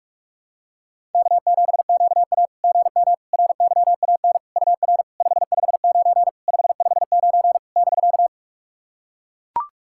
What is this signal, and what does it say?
Morse code, heard as a keyed tone, about 42 wpm, 690 Hz. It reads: K7QA GM RYAN UR 559 559 <BK>
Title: SOTA QSO Practice 45wpm